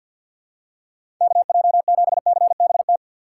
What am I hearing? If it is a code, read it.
Morse code, heard as a keyed tone, about 49 wpm, 690 Hz. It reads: KJ7CBT
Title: USA Call Signs - 6x Character Spacing - 50wpm